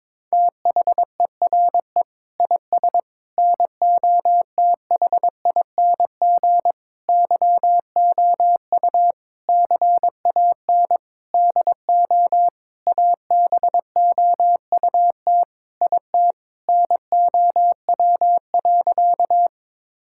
Morse code, heard as a keyed tone, about 22 wpm, 700 Hz. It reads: THERE IS NOTHING YOU CAN DO ABOUT IT NOW.
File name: Sentences with 7-Word Phrases - 22wpm